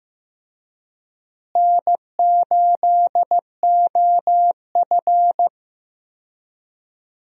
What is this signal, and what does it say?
Morse code, heard as a keyed tone, about 15 wpm, 700 Hz. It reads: N8OF
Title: USA Call Signs - 4 Characters - Repeated 3x - 15wpm